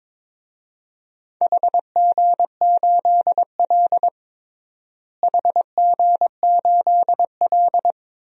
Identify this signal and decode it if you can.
Morse code, heard as a keyed tone, about 22 wpm, 700 Hz. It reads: HG8L HG8L
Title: DXCC-100 Call Signs - 4 Characters - Repeated 3x - 22wpm